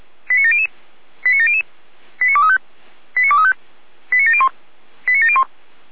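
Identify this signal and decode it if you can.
Five-tone selective call (ZVEI), heard as quick runs of five tones, tones of about 70 ms. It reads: 89700 89700 89235 89235 89701 89701
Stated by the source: ZVEI five-tone selective call received by SDR.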